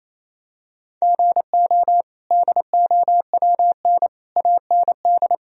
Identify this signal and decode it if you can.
Morse code, heard as a keyed tone, about 28 wpm, 700 Hz. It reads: GO DOWN AND
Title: Sentences from Top 1000 Words - Review Part 1 - 28wpm